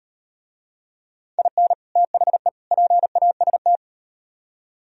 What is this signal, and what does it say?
Morse code, heard as a keyed tone, about 38 wpm, 700 Hz. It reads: IN THE PAST